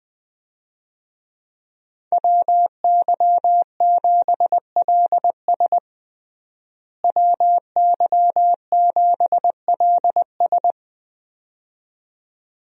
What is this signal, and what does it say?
Morse code, heard as a keyed tone, about 20 wpm, 700 Hz. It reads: WY7LS WY7LS
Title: USA Call Signs - 5 Characters - Repeated 2x - 20wpm